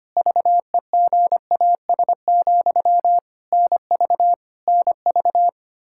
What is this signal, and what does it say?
Morse code, heard as a keyed tone, about 25 wpm, 700 Hz. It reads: VEGAS, NV NV
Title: K1USN SST QSO Practice 25wpm